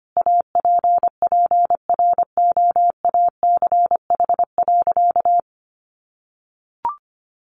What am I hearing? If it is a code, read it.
Morse code, heard as a keyed tone, about 25 wpm, 700 Hz. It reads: APPROACH.
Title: Sentences from Top 1400 Words 25wpm